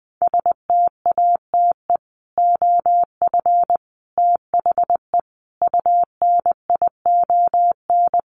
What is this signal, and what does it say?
Morse code, heard as a keyed tone, about 20 wpm, 700 Hz. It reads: STATE OF THE UNION
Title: Sets of 5 Words Encore 20wpm